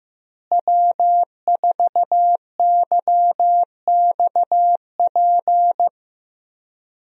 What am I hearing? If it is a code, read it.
Morse code, heard as a keyed tone, about 15 wpm, 700 Hz. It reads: W4YXP
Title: USA Call Signs - 5 Characters - Repeated 3x - 15wpm